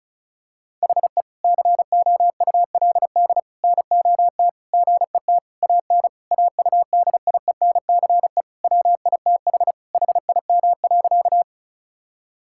Morse code, heard as a keyed tone, about 34 wpm, 700 Hz. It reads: HE COULD NOT GET AN AUDIENCE WITH HIM.
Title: Sentences from Top 1000 Words - Review Part 5 - 35wpm